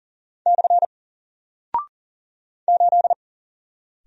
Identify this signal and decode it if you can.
Morse code, heard as a keyed tone, about 40 wpm, 700 Hz. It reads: / 8
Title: Single Character (Letter:Number:Punctuation:Prosign) 40wpm